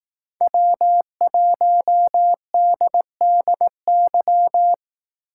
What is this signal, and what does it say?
Morse code, heard as a keyed tone, about 18 wpm, 700 Hz. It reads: W1DDY
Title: Speed Racing - USA 1x3 Call Signs 15wpm